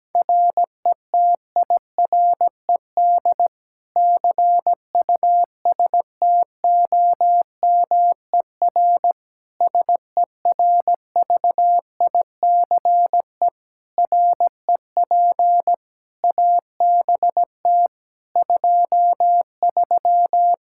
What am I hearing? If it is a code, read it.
Morse code, heard as a keyed tone, about 17 wpm, 700 Hz. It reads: RETIRED CUSTOMER SERVICE REP ABT 23
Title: K1USN SST QSO Practice 17wpm